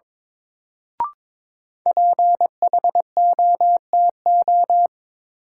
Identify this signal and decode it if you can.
Morse code, heard as a keyed tone, about 22 wpm, 700 Hz. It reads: PHOTO
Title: Sentences from Top 1200 Words - New Word Drill - 22wpm